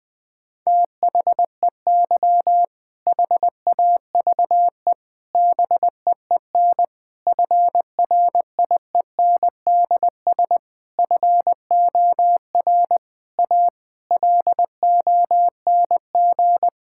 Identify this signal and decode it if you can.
Morse code, heard as a keyed tone, about 21 wpm, 700 Hz. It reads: THEY HAVE BEEN FRIENDS FOR A LONG
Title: Sentences with 6-Word Phrases - Spoken Beginning - 20wpm